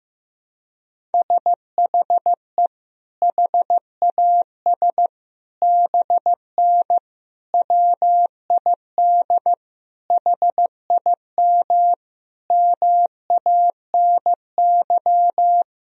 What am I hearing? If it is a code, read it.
Morse code, heard as a keyed tone, about 15 wpm, 700 Hz. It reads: SHE HAS BN WID HIM MANY
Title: Sentences from Top 200 Words and Abbreviations 15wpm